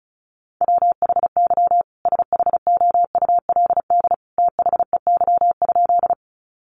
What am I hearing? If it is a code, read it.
Morse code, heard as a keyed tone, about 34 wpm, 700 Hz. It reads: WHY SHOULD THEY?
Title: Sentences from Top 300 Words 35wpm